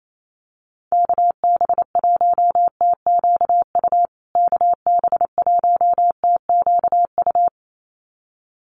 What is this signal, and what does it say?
Morse code, heard as a keyed tone, about 28 wpm, 700 Hz. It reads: KB1TQU KB1TQU
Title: USA Call Signs Repeated 2x 28wpm